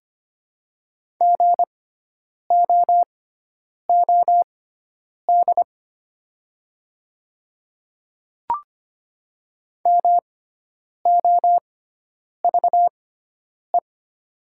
Morse code, heard as a keyed tone, about 25 wpm, spaced as for 9 wpm, 700 Hz. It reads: GOOD MOVE
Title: 6x Character Spacing - Top 300 Words - 25wpm